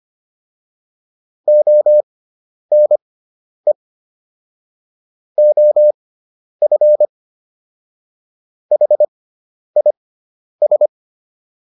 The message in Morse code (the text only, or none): ONE OF HIS